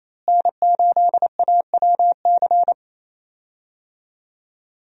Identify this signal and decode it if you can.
Morse code, heard as a keyed tone, about 28 wpm, 700 Hz. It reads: N8AWC
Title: USA Call Signs - 5 Characters - Repeated 2x - 28wpm